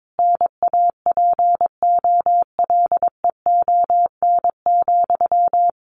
Morse code, heard as a keyed tone, about 22 wpm, 700 Hz. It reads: NAPOLEON,
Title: A Sound Of Thunder 22wpm